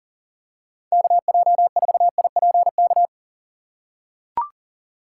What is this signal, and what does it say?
Morse code, heard as a keyed tone, about 40 wpm, 700 Hz. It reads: KJ4IPK
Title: USA Call Signs - 4x Character Spacing - 40wpm